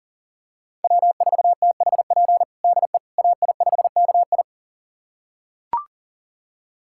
Morse code, heard as a keyed tone, about 37 wpm, 700 Hz. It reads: W4THP DE AI5KI
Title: USA Callsign DE Callsign - 5 Characters - 40wpm